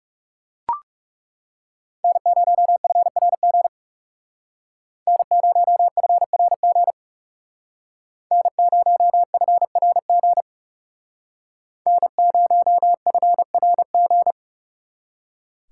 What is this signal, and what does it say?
Morse code, heard as a keyed tone, about 38 wpm, 700 Hz. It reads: N0FRG N0FRG N0FRG N0FRG